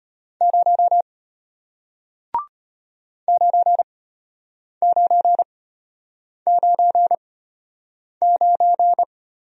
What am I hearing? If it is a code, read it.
Morse code, heard as a keyed tone, about 34 wpm, spaced as for 8 wpm, 700 Hz. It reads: 0 9999